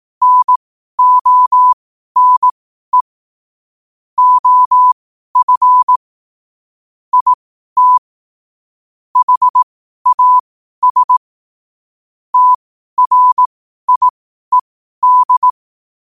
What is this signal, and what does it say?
Morse code, heard as a keyed tone, about 18 wpm, spaced as for 12 wpm, 1000 Hz. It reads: NONE OF IT HAS TRIED
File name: qotdfiles / 12wpm